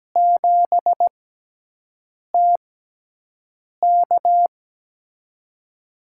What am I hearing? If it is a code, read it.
Morse code, heard as a keyed tone, about 17 wpm, spaced as for 6 wpm, 700 Hz. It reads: 7TK
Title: USA Call Signs - 5 Characters - 6x Character Spacing - 17wpm